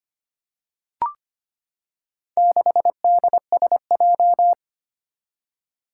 Morse code, heard as a keyed tone, about 25 wpm, 700 Hz. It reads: 6DSJ
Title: Four Challenging Letter-Number Combinations 25wpm